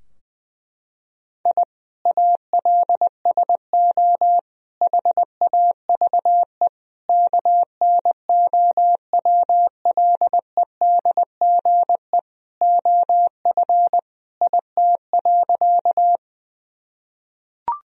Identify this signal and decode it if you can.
Morse code, heard as a keyed tone, about 20 wpm, 700 Hz. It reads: I ALSO HAVE KNOWLEDGE OF IT.
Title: Sentences from Top 1000 Words - Review Part 3 - 20wpm